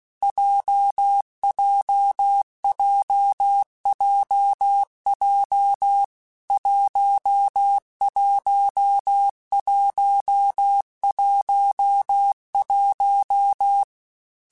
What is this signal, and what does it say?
Morse code, heard as a keyed tone, about 16 wpm, 800 Hz. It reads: JJJJJ 11111
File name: Difficult Combos / Difficult Combos-mp3s / WPJ WPJ1 @15